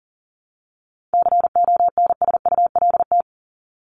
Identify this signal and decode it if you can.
Morse code, heard as a keyed tone, about 40 wpm, 700 Hz. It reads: CONSULT
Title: Top 1751-2000 Words 40wpm